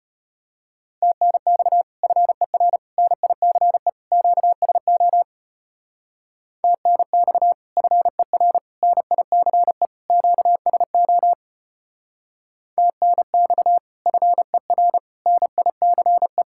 Morse code, heard as a keyed tone, about 33 wpm, 700 Hz. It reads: TNX FER NICE QSO TNX FER NICE QSO TNX FER NICE